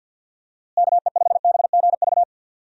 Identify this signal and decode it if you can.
Morse code, heard as a keyed tone, about 52 wpm, 700 Hz. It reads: KE5BGV
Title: USA Call Signs - 6 Characters - Repeated 3x - 50wpm